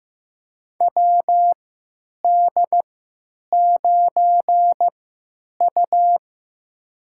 Morse code, heard as a keyed tone, about 15 wpm, spaced as for 9 wpm, 700 Hz. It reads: WD9U
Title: USA Call Signs - 4 Characters - 3x Character Spacing - 15wpm